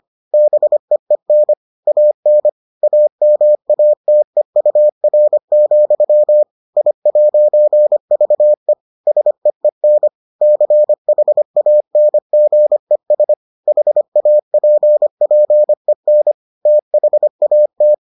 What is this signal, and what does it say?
Morse code, heard as a keyed tone, about 25 wpm, 600 Hz. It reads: BEEN AN AMATEUR, I'VE SEEN CHANGES HAPPEN THAT